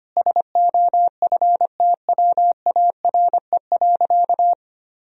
Morse code, heard as a keyed tone, about 25 wpm, 700 Hz. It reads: SOFTWARE.
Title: Sentences from Top 1000 Words - Review Part 3 - 25wpm